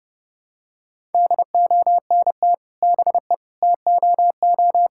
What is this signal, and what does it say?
Morse code, heard as a keyed tone, about 32 wpm, 700 Hz. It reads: DONT BE TOO
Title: Sentences from Top 1000 Words - Review Part 5 - 30wpm